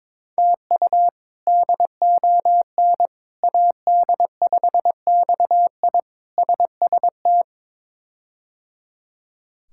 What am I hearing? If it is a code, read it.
Morse code, heard as a keyed tone, about 22 wpm, 700 Hz. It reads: TU DON AD5XI SST
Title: K1USN SST QSO Practice - 22wpm